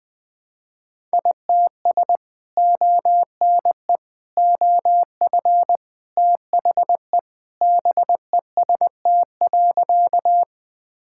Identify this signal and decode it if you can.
Morse code, heard as a keyed tone, about 21 wpm, 700 Hz. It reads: ITS ONE OF THE BEST.
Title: Sentences with 3-Word Phrases - Spoken Beginning - 20wpm